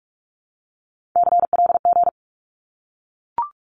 Morse code, heard as a keyed tone, about 44 wpm, 700 Hz. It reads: CLG
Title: Abbreviations 45wpm